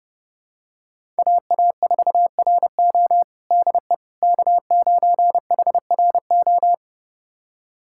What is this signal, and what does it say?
Morse code, heard as a keyed tone, about 30 wpm, 700 Hz. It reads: AA4RO DE K9HRO
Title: USA Callsign DE Callsign - 5 Characters - 30wpm